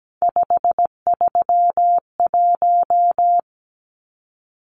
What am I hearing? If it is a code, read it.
Morse code, heard as a keyed tone, about 17 wpm, 700 Hz. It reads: 531
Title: Numbers - 3-Digit 17wpm